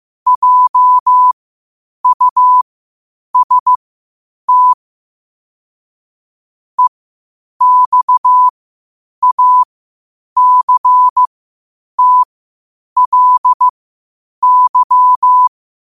Morse code, heard as a keyed tone, about 15 wpm, spaced as for 8 wpm, 1000 Hz. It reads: JUST EXACTLY
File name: qotdfiles / 8wpm